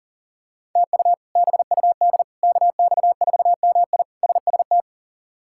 Morse code, heard as a keyed tone, about 39 wpm, 700 Hz. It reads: TU BUD KX4MI SST
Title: K1USN SST QSO Practice - 40wpm